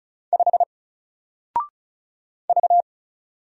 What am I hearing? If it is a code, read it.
Morse code, heard as a keyed tone, about 34 wpm, 700 Hz. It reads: 5 V
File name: Single-Letter-Number-35wpm